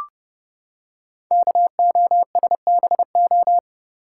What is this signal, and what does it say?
Morse code, heard as a keyed tone, about 30 wpm, 700 Hz. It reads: KOSBO
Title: Code Groups - 5 Letters - 30wpm